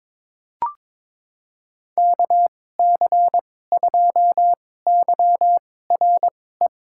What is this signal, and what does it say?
Morse code, heard as a keyed tone, about 22 wpm, spaced as for 16 wpm, 700 Hz. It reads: KC2YRE
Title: USA Call Signs - 6 Characters - 2x Character Spacing - 22wpm